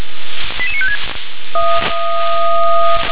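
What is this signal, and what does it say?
Five-tone selective call (ZVEI), heard as quick runs of five tones, tones of about 70 ms. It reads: 88956
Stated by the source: Five-tone ZVEI selective call received on an SDR.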